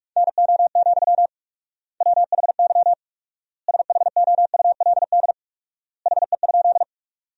Morse code, heard as a keyed tone, about 41 wpm, 700 Hz. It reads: NO, WHY SHOULD HE?
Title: 4x Word-Spacing - Sentences from Top 300 Words - 45wpm